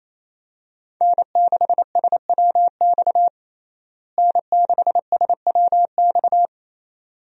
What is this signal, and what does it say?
Morse code, heard as a keyed tone, about 27 wpm, 700 Hz. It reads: N6SWX N6SWX
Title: USA Call Signs - 5 Characters - Repeated 2x - 28wpm